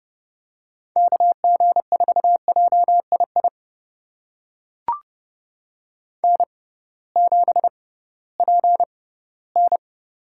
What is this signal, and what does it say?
Morse code, heard as a keyed tone, about 30 wpm, spaced as for 17 wpm, 700 Hz. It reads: KG4JII N7PN
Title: USA Call Signs - 6x Character Spacing - 30wpm